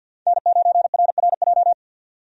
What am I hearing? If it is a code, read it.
Morse code, heard as a keyed tone, about 47 wpm, 700 Hz. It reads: N9RRJ
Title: USA Call Signs - 5 Characters - 50wpm